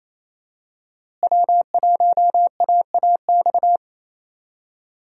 Morse code, heard as a keyed tone, about 28 wpm, 700 Hz. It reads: W1AAX
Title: USA Call Signs - 5 Characters - Repeated 2x - 28wpm